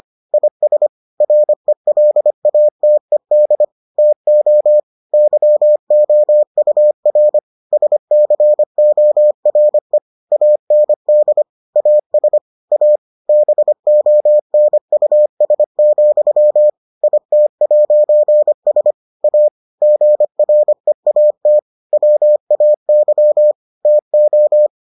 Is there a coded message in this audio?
Yes: Morse code from 0.3 to 24.7 s, heard as a keyed tone, about 25 wpm, 600 Hz. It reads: IS RELATED TO YOUR SCORE AND AS A BONUS, IT'S A GREAT WAY TO